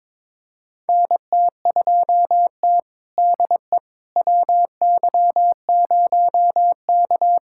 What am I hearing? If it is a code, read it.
Morse code, heard as a keyed tone, about 22 wpm, 700 Hz. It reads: NT2T DE WY0K
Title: USA Callsign DE Callsign - 4 Characters - 22wpm